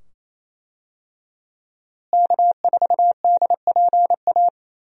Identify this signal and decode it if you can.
Morse code, heard as a keyed tone, about 28 wpm, 700 Hz. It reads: K4DPA
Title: USA Call Signs - 5 Characters - Repeated 2x - 28wpm